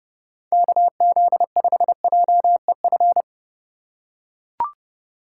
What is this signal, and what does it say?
Morse code, heard as a keyed tone, about 29 wpm, 700 Hz. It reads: KZ5JEF
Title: USA Call Signs - 6 Characters - 4x Character Spacing - 30wpm